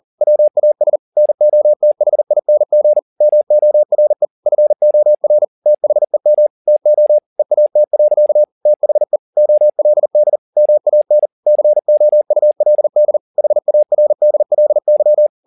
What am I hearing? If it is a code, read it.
Morse code, heard as a keyed tone, about 40 wpm, 600 Hz. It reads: WAS NOTHING MORE FOR THEM TO EAT. THE OLD MAN COULD HARDLY
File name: the old man and the sea 2 40wpm